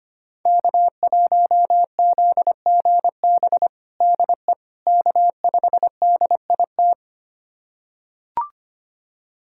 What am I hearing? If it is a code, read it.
Morse code, heard as a keyed tone, about 25 wpm, 700 Hz. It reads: K1ZGB DE K5DIT
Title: USA Callsign DE Callsign - 5 Characters - 25wpm